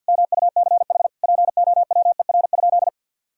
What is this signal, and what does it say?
Morse code, heard as a keyed tone, about 52 wpm, 700 Hz. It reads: MUCH POWER?
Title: Sentences from Top 1000 Words - Review Part 5 - 50wpm